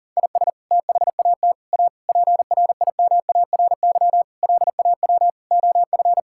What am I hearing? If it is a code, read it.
Morse code, heard as a keyed tone, about 37 wpm, 700 Hz. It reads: IS THAT A PRIMARY LAW OF
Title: Sentences from Top 1000 Words - Review Part 5 - 40wpm